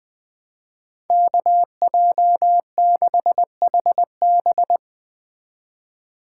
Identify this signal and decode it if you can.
Morse code, heard as a keyed tone, about 22 wpm, 700 Hz. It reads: KJ6HB
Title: Speed Racing - USA Call Signs 20wpm